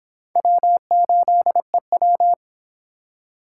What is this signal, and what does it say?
Morse code, heard as a keyed tone, about 25 wpm, 700 Hz. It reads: W8EW